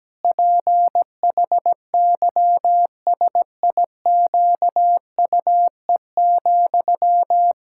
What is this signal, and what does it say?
Morse code, heard as a keyed tone, about 17 wpm, 700 Hz. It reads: PHYSIQUE,
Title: Pangram Challenge v2 - 17wpm